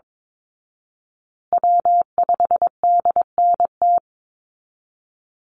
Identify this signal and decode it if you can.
Morse code, heard as a keyed tone, about 22 wpm, 700 Hz. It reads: W5DNT
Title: Call Signs Repeated 3x - 22wpm